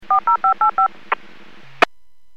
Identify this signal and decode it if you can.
DTMF, heard as telephone touch tones, tones of about 110 ms. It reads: 40585